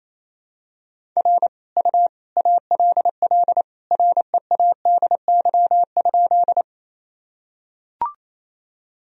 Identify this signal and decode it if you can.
Morse code, heard as a keyed tone, about 28 wpm, 700 Hz. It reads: R U ALL READY?
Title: Sentences from Top 400 Words and Abbreviations 28wpm